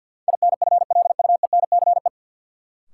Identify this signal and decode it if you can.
Morse code, heard as a keyed tone, about 53 wpm, 700 Hz. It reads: INFLUENCE
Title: Sentences from Top 1100 Words - New Word Drill Alt - 50wpm